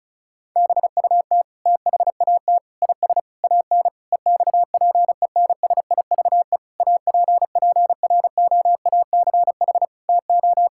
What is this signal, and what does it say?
Morse code, heard as a keyed tone, about 36 wpm, 700 Hz. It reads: BUT THAT IS AN EXPENSIVE APPROACH TO
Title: Sentences from Top 1000 Words - Review Part 6 - 35wpm